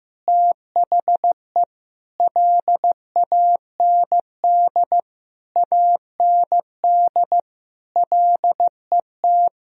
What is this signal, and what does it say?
Morse code, heard as a keyed tone, about 16 wpm, 700 Hz. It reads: THE LAND AND LET
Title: Sentences from Top 1000 Words - Review Part 2 - 15wpm